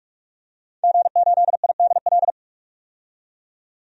Morse code, heard as a keyed tone, about 46 wpm, 700 Hz. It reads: G8IDL
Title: DXCC-100 Call Signs - 5 Characters - 45wpm